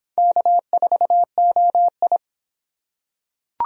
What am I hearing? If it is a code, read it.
Morse code, heard as a keyed tone, about 23 wpm, 700 Hz. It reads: K4OI